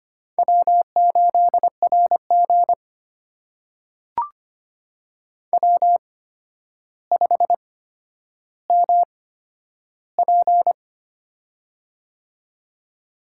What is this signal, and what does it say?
Morse code, heard as a keyed tone, about 25 wpm, spaced as for 11 wpm, 700 Hz. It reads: W8RG W5MP
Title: USA Call Signs - 4 Characters - 8x Character Spacing - 25wpm